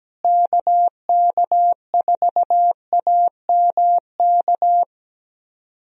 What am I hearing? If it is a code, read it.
Morse code, heard as a keyed tone, about 17 wpm, 700 Hz. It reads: KK4AMK
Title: USA Call Signs - 6 Characters - Repeated 3x - 17wpm